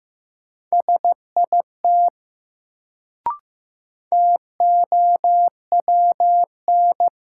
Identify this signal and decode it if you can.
Morse code, heard as a keyed tone, about 15 wpm, 700 Hz. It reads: SIT TOWN
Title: Sentences from Top 500 Words - New Word Drill - 15wpm